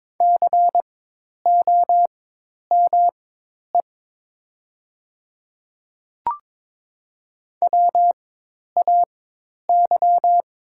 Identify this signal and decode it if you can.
Morse code, heard as a keyed tone, about 22 wpm, spaced as for 10 wpm, 700 Hz. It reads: COME WAY
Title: 4x Character Spacing - Top 100 Words - 22wpm